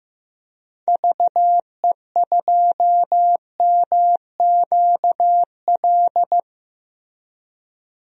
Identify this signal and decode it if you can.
Morse code, heard as a keyed tone, about 15 wpm, 700 Hz. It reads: VE2MQL
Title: DXCC-100 Call Signs - 6 Characters - Repeated 3x - 15wpm